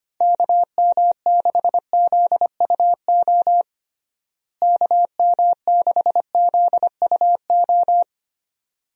Morse code, heard as a keyed tone, about 25 wpm, 700 Hz. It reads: KM6ZUO KM6ZUO
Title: USA Call Signs - 6 Characters - Repeated 2x - 25wpm